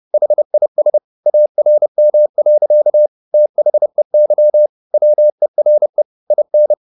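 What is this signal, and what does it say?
Morse code, heard as a keyed tone, about 30 wpm, 600 Hz. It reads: HIS ARM. THEY WERE IN
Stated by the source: Morse code at 30 wpm, the novel Never by Ken Follett in English, from CW med Gnister.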